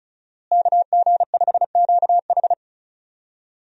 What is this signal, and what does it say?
Morse code, heard as a keyed tone, about 36 wpm, 700 Hz. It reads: KG5QH
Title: USA Call Signs - 5 Characters - Repeated 3x - 35wpm